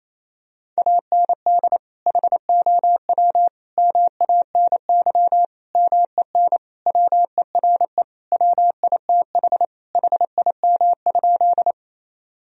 Morse code, heard as a keyed tone, about 28 wpm, 700 Hz. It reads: AND HOW MANY MEN WERE WITH HIM?
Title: Sentences from Top 1000 Words - Review Part 1 - 28wpm